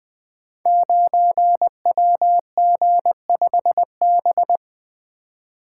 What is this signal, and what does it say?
Morse code, heard as a keyed tone, about 20 wpm, 700 Hz. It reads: 9WG5B
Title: Code Groups - 5 Letter:Numbers - 20wpm